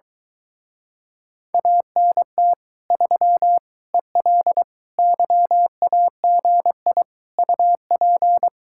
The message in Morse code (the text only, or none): ANT 3 EL YAGI UP